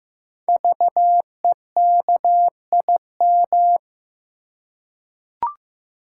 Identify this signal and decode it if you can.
Morse code, heard as a keyed tone, about 15 wpm, 700 Hz. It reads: VEKIM
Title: Code Groups - 5 Letters - 15wpm